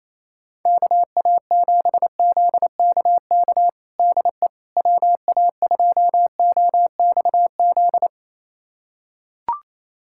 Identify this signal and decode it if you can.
Morse code, heard as a keyed tone, about 28 wpm, 700 Hz. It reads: KA7ZKK DE WA2OXZ
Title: USA Callsign DE Callsign - 6 Characters - 28wpm